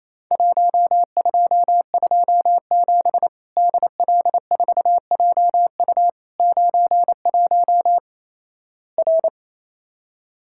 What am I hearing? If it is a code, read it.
Morse code, heard as a keyed tone, about 28 wpm, 700 Hz. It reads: 1227 DL4JU 91 R
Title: Worked All Europe DX Contest - QTC Exchange - 28wpm